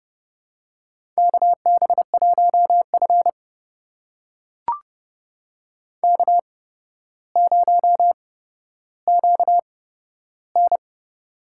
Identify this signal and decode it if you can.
Morse code, heard as a keyed tone, about 30 wpm, spaced as for 13 wpm, 700 Hz. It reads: KB1F K0QN
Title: USA Call Signs - 4 Characters - 8x Character Spacing - 30wpm